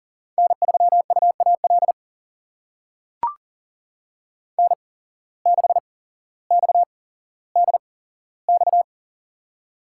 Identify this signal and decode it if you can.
Morse code, heard as a keyed tone, about 40 wpm, spaced as for 17 wpm, 700 Hz. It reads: N3UAL N6XDX
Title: USA Call Signs - 5 Characters - 8x Character Spacing - 40wpm